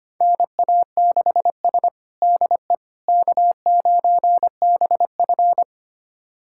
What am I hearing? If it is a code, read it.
Morse code, heard as a keyed tone, about 25 wpm, 700 Hz. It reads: NA6S DE K9BF
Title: USA Callsign DE Callsign - 4 Characters - 25wpm